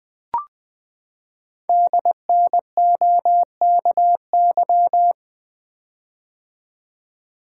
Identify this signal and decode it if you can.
Morse code, heard as a keyed tone, about 20 wpm, 700 Hz. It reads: DNOKY
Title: Code Groups - 5 Letters - 20wpm